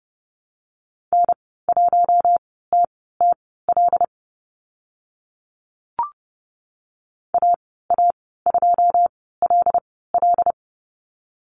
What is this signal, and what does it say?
Morse code, heard as a keyed tone, about 32 wpm, spaced as for 17 wpm, 700 Hz. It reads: N1TTL AA2LL
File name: Lesson 14g - Number 2 - Call Signs 30wpm